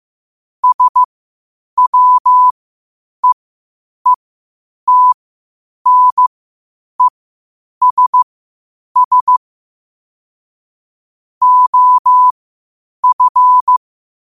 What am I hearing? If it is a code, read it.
Morse code, heard as a keyed tone, about 15 wpm, spaced as for 8 wpm, 1000 Hz. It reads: SWEETNESS OF
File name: qotdfiles / 8wpm